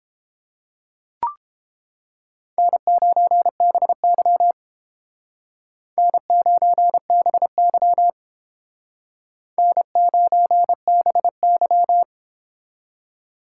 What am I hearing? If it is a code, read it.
Morse code, heard as a keyed tone, about 30 wpm, 700 Hz. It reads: N9BY N9BY N9BY